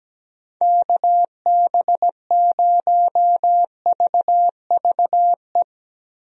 Morse code, heard as a keyed tone, about 17 wpm, 700 Hz. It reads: KB0VVE
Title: USA Call Signs - 6 Characters - 17wpm